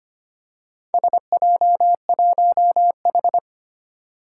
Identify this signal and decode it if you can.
Morse code, heard as a keyed tone, about 25 wpm, 700 Hz. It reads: SJ1H
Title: Four Challenging Letter-Number Combinations 25wpm